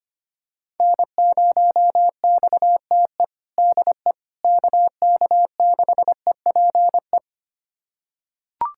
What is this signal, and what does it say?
Morse code, heard as a keyed tone, about 25 wpm, 700 Hz. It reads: N0XTE DE KK6EPE
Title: USA Callsign DE Callsign 25wpm